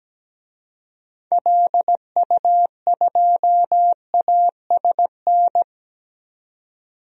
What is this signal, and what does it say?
Morse code, heard as a keyed tone, about 17 wpm, 700 Hz. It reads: LU2ASN
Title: DXCC-100 Call Signs - 6 Characters - 17wpm